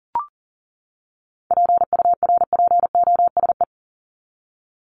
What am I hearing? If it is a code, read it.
Morse code, heard as a keyed tone, about 40 wpm, 700 Hz. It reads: PURPOSE
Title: Top 751-1000 Words 40wpm